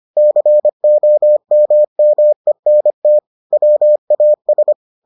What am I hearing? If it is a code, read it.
Morse code, heard as a keyed tone, about 25 wpm, 600 Hz. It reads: COMMENT WAS